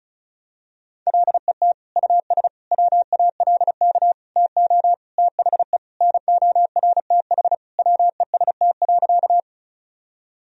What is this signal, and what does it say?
Morse code, heard as a keyed tone, about 34 wpm, 700 Hz. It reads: LET US WALK TO THE NORTH WEST.
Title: Sentences from Top 1000 Words - Review Part 2 - 35wpm